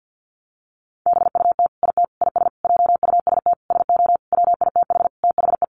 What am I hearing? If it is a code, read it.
Morse code, heard as a keyed tone, about 50 wpm, 700 Hz. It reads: BUT IT IS JUST SO WITH THE
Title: Sentences from Top 200 Words 50wpm